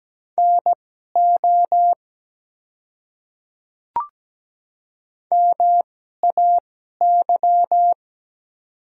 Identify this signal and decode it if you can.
Morse code, heard as a keyed tone, about 17 wpm, spaced as for 12 wpm, 700 Hz. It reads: NO MAY
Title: 2x Character Spacing - Top 100 Words - 17wpm